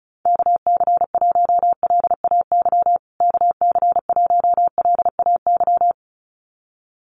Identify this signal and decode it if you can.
Morse code, heard as a keyed tone, about 34 wpm, 700 Hz. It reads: KC1LAY KC1LAY
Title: USA Call Signs Repeated 2x 35wpm